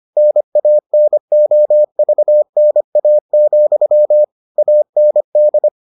Morse code, heard as a keyed tone, about 25 wpm, 600 Hz. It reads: NANOVNA, AND